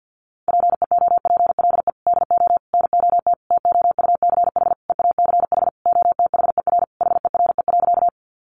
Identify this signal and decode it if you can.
Morse code, heard as a keyed tone, about 50 wpm, 700 Hz. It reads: PEOPLE DO NOT TOUCH EACH OTHER HERE.
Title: Sentences from Top 1200 Words 50wpm